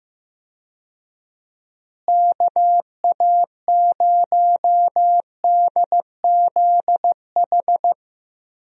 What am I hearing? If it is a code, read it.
Morse code, heard as a keyed tone, about 15 wpm, 700 Hz. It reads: KA0DZH
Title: USA Call Signs - 6 Characters - Repeated 3x - 15wpm